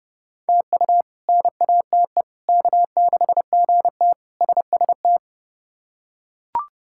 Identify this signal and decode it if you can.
Morse code, heard as a keyed tone, about 30 wpm, 700 Hz. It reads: TU NATE K6GT SST
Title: K1USN SST QSO Practice - 30wpm